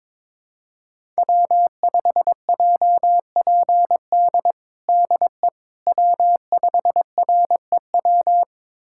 Morse code, heard as a keyed tone, about 22 wpm, 700 Hz. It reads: W5JPD DE W5REW
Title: USA Callsign DE Callsign - 5 Characters - 22wpm